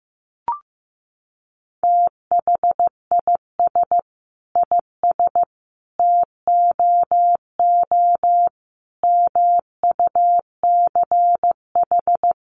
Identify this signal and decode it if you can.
Morse code, heard as a keyed tone, about 15 wpm, 700 Hz. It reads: THIS IS TOO MUCH
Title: A Sound Of Thunder 15wpm